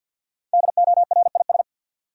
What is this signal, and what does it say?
Morse code, heard as a keyed tone, about 52 wpm, 700 Hz. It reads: DORIS
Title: USA Common Names - 50wpm